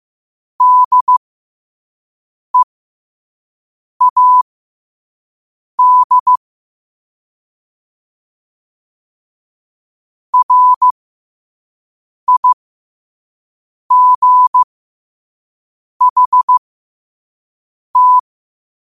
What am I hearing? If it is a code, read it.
Morse code, heard as a keyed tone, about 15 wpm, spaced as for 5 wpm, 1000 Hz. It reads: DEAD RIGHT